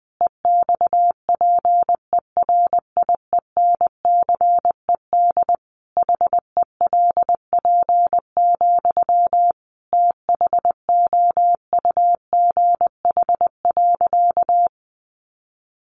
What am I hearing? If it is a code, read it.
Morse code, heard as a keyed tone, about 20 wpm, 700 Hz. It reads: EXPERIENCED HELP, THOUGH.
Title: Sentences from Top 1100 Words 20wpm